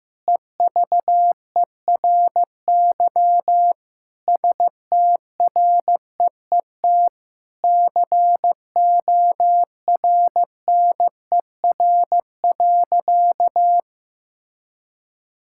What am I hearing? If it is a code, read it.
Morse code, heard as a keyed tone, about 15 wpm, 700 Hz. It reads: EVERY STREET CORNER.
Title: Sentences from Top 1600 Words 15wpm